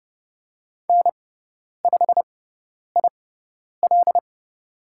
Morse code, heard as a keyed tone, about 30 wpm, spaced as for 10 wpm, 700 Hz. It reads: N5IL
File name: Lesson 16g - Letter c - Call Signs 30wpm